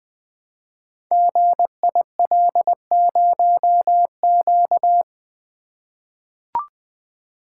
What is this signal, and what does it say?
Morse code, heard as a keyed tone, about 20 wpm, 700 Hz. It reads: GIL0Q
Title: Code Groups - 5 Letter:Numbers - 20wpm